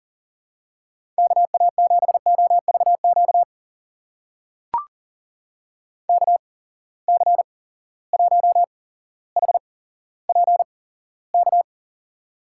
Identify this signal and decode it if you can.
Morse code, heard as a keyed tone, about 40 wpm, spaced as for 17 wpm, 700 Hz. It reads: KA7OVQ KC1HPK
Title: USA Call Signs - 6 Characters - 8x Character Spacing - 40wpm